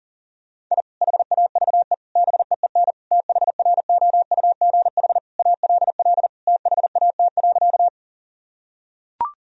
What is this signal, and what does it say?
Morse code, heard as a keyed tone, about 40 wpm, 700 Hz. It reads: I HAVE BEEN THROUGH ALL THAT.
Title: Sentences from Top 1000 Words - Review Part 1 - 40wpm